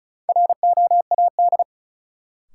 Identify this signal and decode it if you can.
Morse code, heard as a keyed tone, about 34 wpm, 700 Hz. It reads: ROAD
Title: Sets of 5 Words - New Word Drill - 35wpm